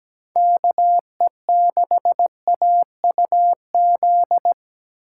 Morse code, heard as a keyed tone, about 20 wpm, 700 Hz. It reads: KE6AUZ
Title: Speed Racing - USA 2x3 Call Signs 17wpm